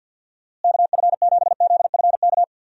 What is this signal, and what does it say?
Morse code, heard as a keyed tone, about 50 wpm, 700 Hz. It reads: KF7ZFK
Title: USA Call Signs - 6 Characters - Repeated 3x - 50wpm